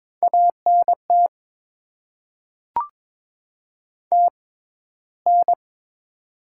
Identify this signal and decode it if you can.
Morse code, heard as a keyed tone, about 22 wpm, spaced as for 14 wpm, 700 Hz. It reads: ANT TN
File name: Top 20 QSO Elements - 6x Character Spacing - 22wpm